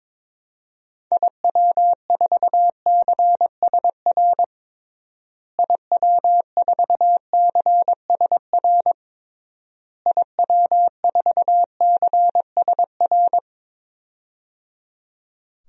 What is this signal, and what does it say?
Morse code, heard as a keyed tone, about 22 wpm, 700 Hz. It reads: IW4CSR IW4CSR IW4CSR
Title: DXCC-100 Call Signs - 6 Characters - Repeated 3x - 22wpm